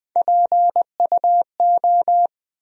Morse code, heard as a keyed tone, about 21 wpm, 700 Hz. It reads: PUO
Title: USA Call Sign Postfixes - 3 Letters 20wpm